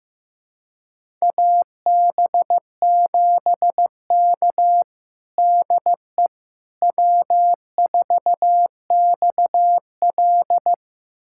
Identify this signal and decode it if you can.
Morse code, heard as a keyed tone, about 15 wpm, 700 Hz. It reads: AB7K DE W4XL
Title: USA Callsign DE Callsign - 4 Characters - 15wpm